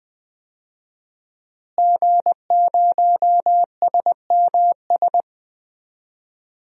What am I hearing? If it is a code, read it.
Morse code, heard as a keyed tone, about 20 wpm, 700 Hz. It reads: G0SMS
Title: DXCC-100 Call Signs - 5 Characters - Repeated 2x - 20wpm